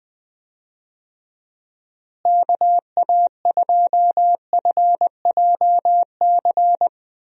Morse code, heard as a keyed tone, about 21 wpm, 700 Hz. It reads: KA2FJC
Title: USA Call Signs - 6 Characters - Repeated 2x - 20wpm